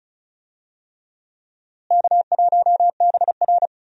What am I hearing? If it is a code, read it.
Morse code, heard as a keyed tone, about 36 wpm, 700 Hz. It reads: K1BR
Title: USA Call Signs - 4 Characters - Repeated 3x - 35wpm